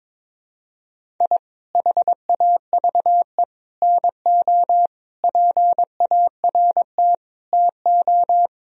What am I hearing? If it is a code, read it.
Morse code, heard as a keyed tone, about 21 wpm, 700 Hz. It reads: I HAVE NO PART TO
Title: Sentences from Top 1000 Words - Review Part 1 - 22wpm